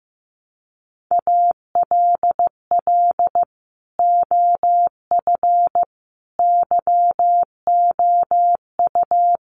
Morse code, heard as a keyed tone, about 14 wpm, 700 Hz. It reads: ALL OF YOU
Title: Sentences from Top 1200 Words 15wpm